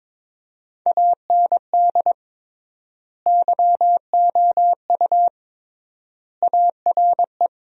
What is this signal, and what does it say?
Morse code, heard as a keyed tone, about 22 wpm, 700 Hz. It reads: AND YOU ARE
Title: 3x Word-Spacing - Sentences from Top 400 Words - 22wpm